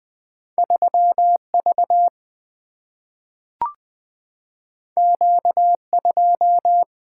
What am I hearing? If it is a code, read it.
Morse code, heard as a keyed tone, about 20 wpm, 700 Hz. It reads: 3V Q2
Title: Single-Letter Single-Number 20wpm